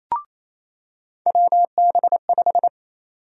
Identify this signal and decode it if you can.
Morse code, heard as a keyed tone, about 28 wpm, 700 Hz. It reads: WB5
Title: USA Call Sign Prefixes - 28wpm